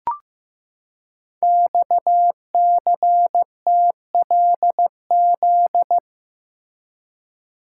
Morse code, heard as a keyed tone, about 15 wpm, 700 Hz. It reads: XCTLZ
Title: Code Groups - 5 Letters - 15wpm